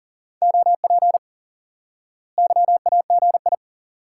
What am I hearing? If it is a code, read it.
Morse code, heard as a keyed tone, about 37 wpm, 700 Hz. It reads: OP YAGI
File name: IQE Course 3 - Lesson 89 - YAGI - Two Elements - 40wpm